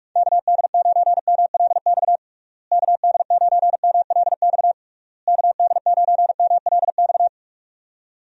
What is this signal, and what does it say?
Morse code, heard as a keyed tone, about 44 wpm, 700 Hz. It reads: KD9MLX KD9MLX KD9MLX
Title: USA Call Signs - 6 Characters - Repeated 3x - 45wpm